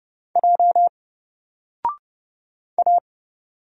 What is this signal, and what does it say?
Morse code, heard as a keyed tone, about 30 wpm, 700 Hz. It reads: J A
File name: Single-Letter-Number-30wpm